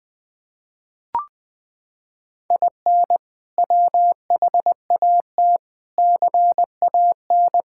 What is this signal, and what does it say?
Morse code, heard as a keyed tone, about 20 wpm, 700 Hz. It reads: IN WHAT CAN
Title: Sets of 8 Words - 20wpm